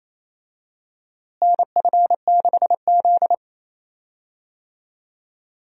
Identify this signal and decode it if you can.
Morse code, heard as a keyed tone, about 28 wpm, 700 Hz. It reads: NF6Z
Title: USA Call Signs - 4 Characters - Repeated 3x - 28wpm